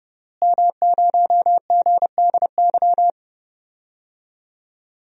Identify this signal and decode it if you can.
Morse code, heard as a keyed tone, about 30 wpm, 700 Hz. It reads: M0GDY
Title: DXCC-100 Call Signs - 5 Characters - Repeated 3x - 30wpm